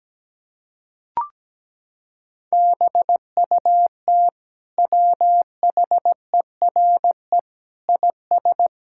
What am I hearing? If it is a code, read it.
Morse code, heard as a keyed tone, about 17 wpm, 700 Hz. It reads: BUT WHERE IS
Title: Sentences from Top 1000 Words - Review Part 5 - 17wpm